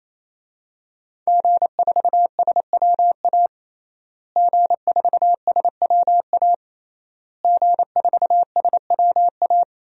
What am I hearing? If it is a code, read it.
Morse code, heard as a keyed tone, about 28 wpm, 700 Hz. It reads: G4SWA G4SWA G4SWA
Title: DXCC-100 Call Signs - 5 Characters - Repeated 3x - 28wpm